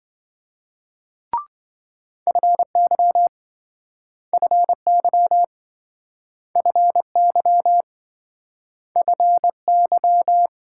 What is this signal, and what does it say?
Morse code, heard as a keyed tone, about 27 wpm, 700 Hz. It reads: FY FY FY FY